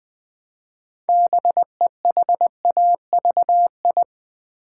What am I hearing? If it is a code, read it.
Morse code, heard as a keyed tone, about 20 wpm, 700 Hz. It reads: BEHAVI
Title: Build A Word - 3 Letters At A Time - 9-Letter Words - 20wpm